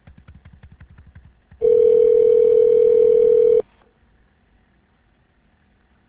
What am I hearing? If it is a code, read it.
Ringback tone (North America), 440 + 480 Hz.